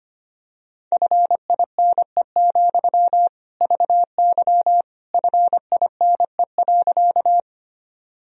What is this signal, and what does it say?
Morse code, heard as a keyed tone, about 25 wpm, 700 Hz. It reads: FINE, VY FINE.
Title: Sentences from Top 500 Words and Abbreviations 25wpm